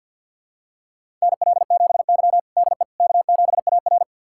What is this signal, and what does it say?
Morse code, heard as a keyed tone, about 50 wpm, 700 Hz. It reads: NL7Y DE K7AR
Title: USA Callsign DE Callsign - 4 Characters - 50wpm